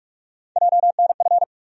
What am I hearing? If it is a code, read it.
Morse code, heard as a keyed tone, about 43 wpm, 700 Hz. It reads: JNF
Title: USA Call Sign Postfixes - 3 Letters 45wpm